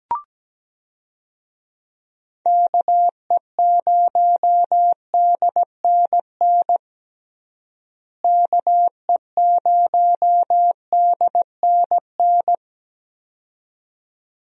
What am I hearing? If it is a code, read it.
Morse code, heard as a keyed tone, about 17 wpm, 700 Hz. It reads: KE0DNN KE0DNN
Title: USA Call Signs - 6 Characters - Repeated 2x - 17wpm